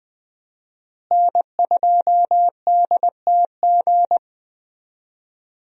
Morse code, heard as a keyed tone, about 20 wpm, 700 Hz. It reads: N2DTG
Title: USA Call Signs - 5 Characters - Repeated 3x - 20wpm